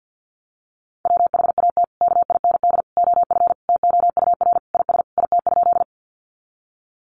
Morse code, heard as a keyed tone, about 48 wpm, 700 Hz. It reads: WHAT KIND OF TOUR IS IT?
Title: Sentences from Top 1400 Words 50wpm